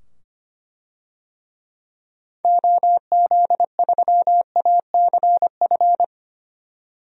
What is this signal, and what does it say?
Morse code, heard as a keyed tone, about 25 wpm, 700 Hz. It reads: OZ3ACF
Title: DXCC-100 Call Signs - 6 Characters - 25wpm